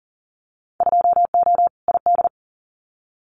Morse code, heard as a keyed tone, about 40 wpm, 700 Hz. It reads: 2O ID
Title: Winter Field Day Exchanges 40wpm